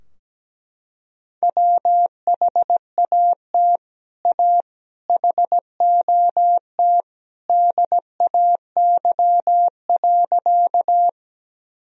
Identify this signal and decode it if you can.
Morse code, heard as a keyed tone, about 17 wpm, 700 Hz. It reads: WHAT A HOT DAY.
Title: Sentences from Top 1000 Words - Review Part 3 - 17wpm